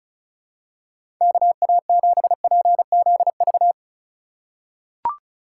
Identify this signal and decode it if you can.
Morse code, heard as a keyed tone, about 36 wpm, 700 Hz. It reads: KA7PZV
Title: USA Call Signs - 6 Characters - 3x Character Spacing - 35wpm